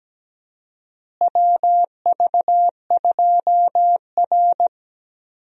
Morse code, heard as a keyed tone, about 17 wpm, 700 Hz. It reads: WV2R
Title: USA Call Signs - 4 Characters - Repeated 2x - 17wpm